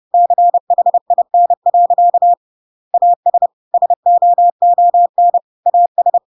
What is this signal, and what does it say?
Morse code, heard as a keyed tone, about 30 wpm, 700 Hz. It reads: CHIN. AS SOON AS
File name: Pinocchio / Pinocchio ch1 30 30 ik1hge